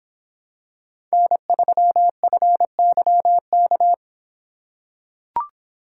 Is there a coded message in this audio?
Yes: Morse code from 1.1 to 3.9 s, heard as a keyed tone, about 25 wpm, 700 Hz. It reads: N3FYK